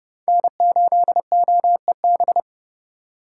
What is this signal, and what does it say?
Morse code, heard as a keyed tone, about 30 wpm, 700 Hz. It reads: N8OEB
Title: USA Call Signs - 5 Characters - 4x Character Spacing - 30wpm